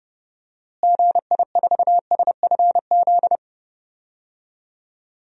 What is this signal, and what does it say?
Morse code, heard as a keyed tone, about 31 wpm, 700 Hz. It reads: GI4SFZ
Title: DXCC-100 Call Signs - 6 Characters - Repeated 2x - 30wpm